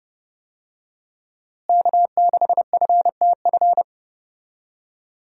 Morse code, heard as a keyed tone, about 30 wpm, 700 Hz. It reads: K6FTF
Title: USA Call Signs - 5 Characters - Repeated 3x - 30wpm